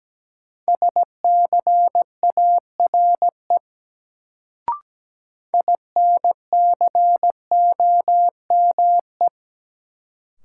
Morse code, heard as a keyed tone, about 17 wpm, 700 Hz. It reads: SCARE INCOME
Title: Sentences from Top 1000 Words - Review Part 5 - New Word Drill - 17wpm